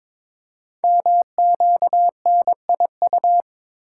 Morse code, heard as a keyed tone, about 22 wpm, 700 Hz. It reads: MQNIU
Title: Code Groups - 5 Letters - 22wpm